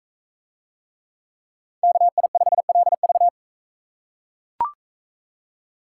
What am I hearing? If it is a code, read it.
Morse code, heard as a keyed tone, about 42 wpm, 700 Hz. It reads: KI5LV